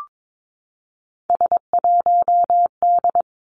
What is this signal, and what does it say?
Morse code, heard as a keyed tone, about 22 wpm, 700 Hz. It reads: S1D
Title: Three Challenging Letter-Number Combinations 22wpm